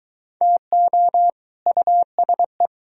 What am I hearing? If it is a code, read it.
Morse code, heard as a keyed tone, about 22 wpm, 700 Hz. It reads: TO USE